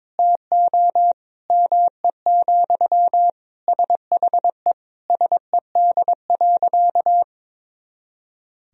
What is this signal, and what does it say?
Morse code, heard as a keyed tone, about 21 wpm, 700 Hz. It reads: TO ME, SHE SED.
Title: Sentences from Top 500 Words and Abbreviations 22wpm